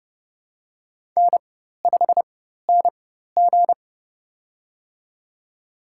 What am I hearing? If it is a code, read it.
Morse code, heard as a keyed tone, about 30 wpm, spaced as for 14 wpm, 700 Hz. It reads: N5NG
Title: USA Call Signs - 4 Characters - 4x Character Spacing - 30wpm